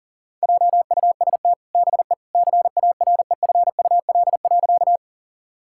Morse code, heard as a keyed tone, about 40 wpm, 700 Hz. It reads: JUST BE CAREFUL.
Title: Sentences from Top 1600 Words 40wpm